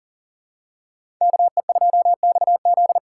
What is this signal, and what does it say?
Morse code, heard as a keyed tone, about 39 wpm, 700 Hz. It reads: KE2XZ
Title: USA Call Signs - 3x Character Spacing - 40wpm